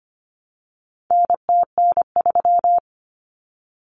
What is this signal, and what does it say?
Morse code, heard as a keyed tone, about 25 wpm, 700 Hz. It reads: NTN3
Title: Common Cut Numbers - 4-Digit 25wpm